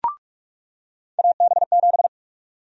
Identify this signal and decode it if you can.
Morse code, heard as a keyed tone, about 46 wpm, 700 Hz. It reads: AB7
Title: USA Call Sign Prefixes - 45wpm